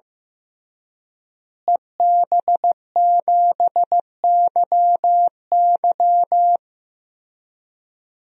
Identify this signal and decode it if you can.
Morse code, heard as a keyed tone, about 15 wpm, 700 Hz. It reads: EB7YY
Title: DXCC-100 Call Signs - 5 Characters - Repeated 3x - 15wpm